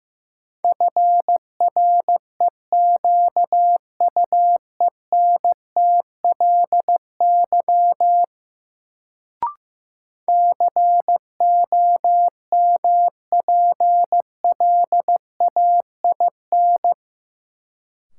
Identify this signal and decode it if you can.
Morse code, heard as a keyed tone, about 15 wpm, 700 Hz. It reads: FREQUENTLY COMPLAIN
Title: Sentences from Top 1300 Words - New Word Drill Alt - 15wpm